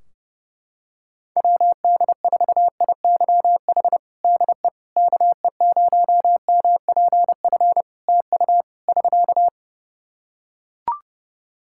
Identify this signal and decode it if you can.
Morse code, heard as a keyed tone, about 32 wpm, 700 Hz. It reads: WD4IYH DE KE0MPF TU <SK>
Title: Practice QSO - Exchange 4 - Ending - 30wpm